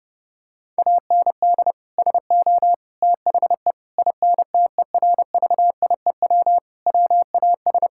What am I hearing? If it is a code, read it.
Morse code, heard as a keyed tone, about 30 wpm, 700 Hz. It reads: AND SO THE INTERVIEW WAS
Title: Sentences from Top 1600 Words 30wpm